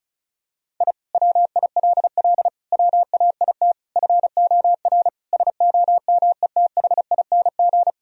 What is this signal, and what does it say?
Morse code, heard as a keyed tone, about 34 wpm, 700 Hz. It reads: I WILL WAIT FOR SOMETHING
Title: Sentences from Top 1000 Words - Review Part 1 - 35wpm